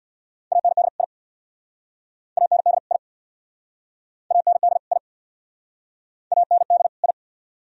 Morse code, heard as a keyed tone, about 67 wpm, 700 Hz. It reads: AND I AND I AND I AND I